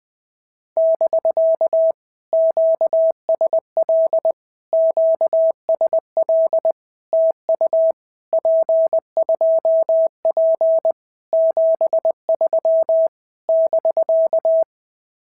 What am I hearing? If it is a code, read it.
Morse code, heard as a keyed tone, about 18 wpm, 650 Hz. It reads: <BK> QSL QSL TU P2P 73 <BK>
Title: POTA QSO Practice - 20wpm